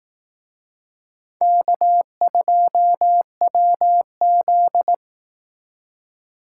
Morse code, heard as a keyed tone, about 18 wpm, 700 Hz. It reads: K2WZ
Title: Speed Racing - USA 1x2 Call Signs 15wpm